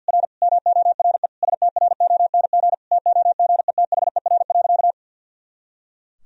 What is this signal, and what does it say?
Morse code, heard as a keyed tone, about 47 wpm, 700 Hz. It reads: R MORE STRONG TOGETHER.
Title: Sentences from Top 500 Words and Abbreviations 50wpm